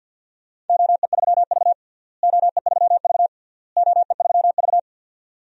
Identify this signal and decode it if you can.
Morse code, heard as a keyed tone, about 47 wpm, 700 Hz. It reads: OE3V OE3V OE3V
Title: DXCC-100 Call Signs - 4 Characters - Repeated 3x - 50wpm